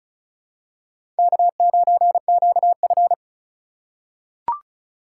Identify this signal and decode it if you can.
Morse code, heard as a keyed tone, about 36 wpm, 700 Hz. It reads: K9QF
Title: USA Call Signs - 4 Characters - 3x Character Spacing - 35wpm